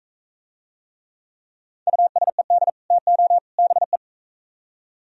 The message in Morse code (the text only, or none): USED TO BE